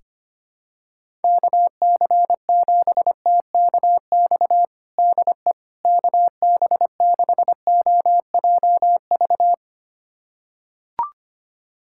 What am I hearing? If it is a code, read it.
Morse code, heard as a keyed tone, about 25 wpm, 700 Hz. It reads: KC7TKX DE KB6OJV
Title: USA Callsign DE Callsign - 6 Characters - 25wpm